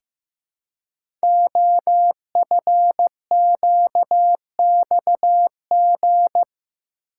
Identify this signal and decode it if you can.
Morse code, heard as a keyed tone, about 15 wpm, 700 Hz. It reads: OFQXG
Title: Code Groups - 5 Letters - 15wpm